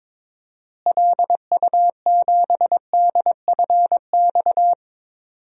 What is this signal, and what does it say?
Morse code, heard as a keyed tone, about 22 wpm, 700 Hz. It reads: LU7DFX
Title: DXCC-100 Call Signs - 6 Characters - Repeated 2x - 22wpm